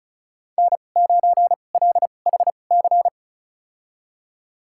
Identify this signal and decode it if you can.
Morse code, heard as a keyed tone, about 36 wpm, spaced as for 25 wpm, 700 Hz. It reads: N9LHC
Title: USA Call Signs - 5 Characters - 2x Character Spacing - 35wpm